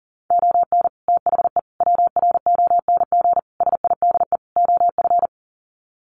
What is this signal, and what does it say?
Morse code, heard as a keyed tone, about 40 wpm, 700 Hz. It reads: ON THE WRONG SIDE OF
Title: Sets of 5 Words Encore 40wpm